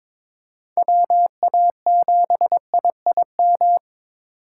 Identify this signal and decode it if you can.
Morse code, heard as a keyed tone, about 26 wpm, 700 Hz. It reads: WA7IIM
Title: Speed Racing - USA 2x3 Call Signs 22wpm